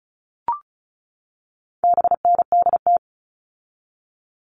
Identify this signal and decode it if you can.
Morse code, heard as a keyed tone, about 34 wpm, 700 Hz. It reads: BNDT
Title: All Cut Numbers - 4-Digit 35wpm